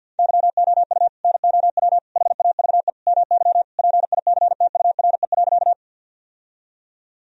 Morse code, heard as a keyed tone, about 46 wpm, 700 Hz. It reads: YOU NOW HAVE MY PICTURE.
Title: Sentences from Top 1000 Words - Review Part 1 - 50wpm